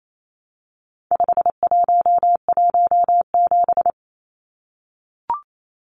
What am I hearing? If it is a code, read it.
Morse code, heard as a keyed tone, about 28 wpm, 700 Hz. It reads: 5117
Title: Numbers - 4-Digit 28wpm